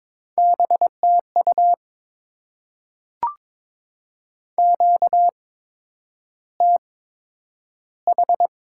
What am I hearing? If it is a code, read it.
Morse code, heard as a keyed tone, about 22 wpm, spaced as for 9 wpm, 700 Hz. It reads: BTU QTH
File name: Top 20 QSO Elements - 8x Character Spacing - 22wpm